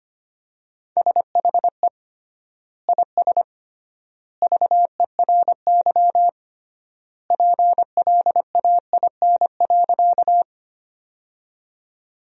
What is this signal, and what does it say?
Morse code, heard as a keyed tone, about 25 wpm, 700 Hz. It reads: SHE IS VERY PLAIN.
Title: 3x Word-Spacing - Sentences from Top 400 Words - 25wpm